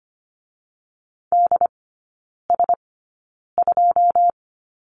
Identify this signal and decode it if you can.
Morse code, heard as a keyed tone, about 25 wpm, spaced as for 9 wpm, 690 Hz. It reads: DS2
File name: Lesson 14e - Number 2 - Three Characters 25wpm